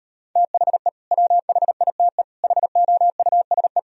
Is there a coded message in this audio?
Yes: Morse code from 0.4 to 3.8 s, heard as a keyed tone, about 38 wpm, 700 Hz. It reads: THE WHITE HOUSE